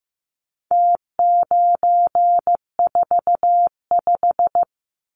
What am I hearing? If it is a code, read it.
Morse code, heard as a keyed tone, about 15 wpm, 700 Hz. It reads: T945
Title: Numbers - Contest Style - 4-Digit 15wpm